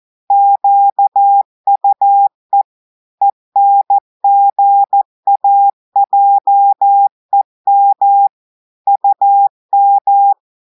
Morse code, heard as a keyed tone, about 14 wpm, 800 Hz. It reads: QUE ENGAJEM UM
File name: audios / portugais / portugais texte en clair 2 vitesse 14 mots minutes